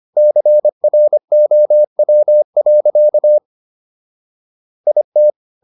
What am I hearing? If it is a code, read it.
Morse code, heard as a keyed tone, about 25 wpm, 600 Hz. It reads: CROW. IT